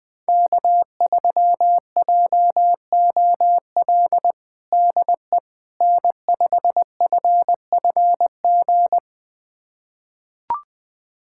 Morse code, heard as a keyed tone, about 20 wpm, 700 Hz. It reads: K3JOL DE N5FFG
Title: USA Callsign DE Callsign - 5 Characters - 20wpm